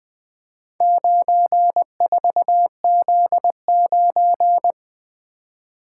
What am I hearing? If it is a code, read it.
Morse code, heard as a keyed tone, about 20 wpm, 700 Hz. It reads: 94Z9
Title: Four Letter-Number Combinations - 20wpm